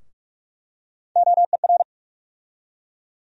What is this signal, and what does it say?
Morse code, heard as a keyed tone, about 46 wpm, 700 Hz. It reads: OER
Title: USA Call Sign Postfixes - 3 Letters 45wpm